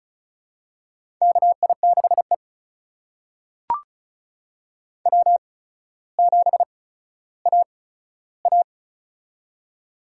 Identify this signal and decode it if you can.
Morse code, heard as a keyed tone, about 36 wpm, spaced as for 15 wpm, 700 Hz. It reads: KI6E W7AA
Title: USA Call Signs - 8x Character Spacing - 35wpm